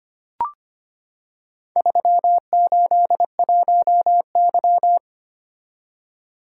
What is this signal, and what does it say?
Morse code, heard as a keyed tone, about 25 wpm, 700 Hz. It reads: 381Y
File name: practice / 25wpm / Four Letter-Number Combinations - 25wpm